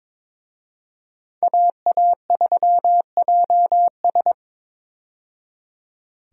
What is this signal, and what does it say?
Morse code, heard as a keyed tone, about 21 wpm, 700 Hz. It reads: AA3JS
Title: USA Call Signs - 5 Characters - 22wpm